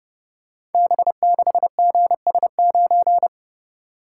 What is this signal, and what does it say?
Morse code, heard as a keyed tone, about 30 wpm, 700 Hz. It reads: B6GS9
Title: Code Groups - 5 Letter:Numbers - 30wpm